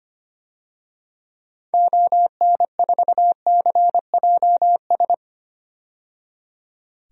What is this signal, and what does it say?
Morse code, heard as a keyed tone, about 25 wpm, 700 Hz. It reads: ON4CJS
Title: DXCC-100 Call Signs - 6 Characters - 25wpm